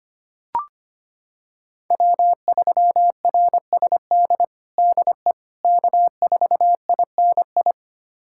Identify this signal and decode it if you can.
Morse code, heard as a keyed tone, about 25 wpm, 700 Hz. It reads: W3RSD DE K4INI
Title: USA Callsign DE Callsign - 5 Characters - 25wpm